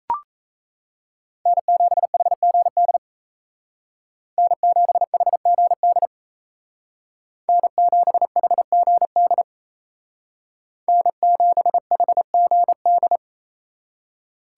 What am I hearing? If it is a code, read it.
Morse code, heard as a keyed tone, about 36 wpm, 700 Hz. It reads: N7HGD N7HGD N7HGD N7HGD